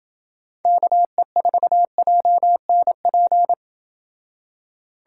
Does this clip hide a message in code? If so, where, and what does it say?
Yes: Morse code from 0.7 to 3.5 s, heard as a keyed tone, about 27 wpm, 700 Hz. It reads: KE4JNP